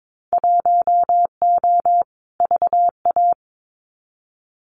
Morse code, heard as a keyed tone, about 22 wpm, 700 Hz. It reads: 1O VA
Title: Winter Field Day Exchanges 22wpm